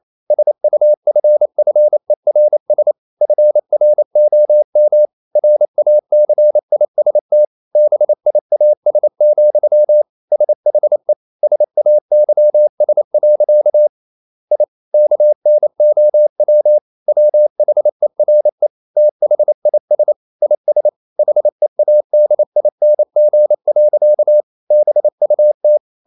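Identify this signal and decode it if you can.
Morse code, heard as a keyed tone, about 28 wpm, 600 Hz. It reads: SUFFERS FROM RACIST BIAS, SHE SAYS. I KNOW WHERE THIS IS HEADING. BUT
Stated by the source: Morse code practice at 28 wpm, text of Ken Follett's novel Never in English, from CW med Gnister.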